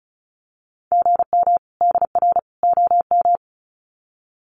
Morse code, heard as a keyed tone, about 34 wpm, 700 Hz. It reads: GM DR OM
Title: Common QSO Phrases 35wpm